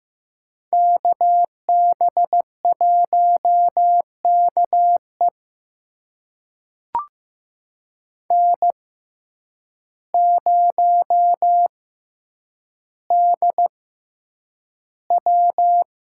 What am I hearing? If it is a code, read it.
Morse code, heard as a keyed tone, about 15 wpm, spaced as for 8 wpm, 700 Hz. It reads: KB1KE N0DW
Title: USA Call Signs - 5 Characters - 6x Character Spacing - 15wpm